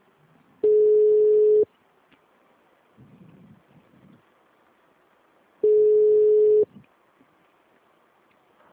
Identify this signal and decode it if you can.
Ringback tone (Europe, 425 Hz).
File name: out- -2015- - - .2056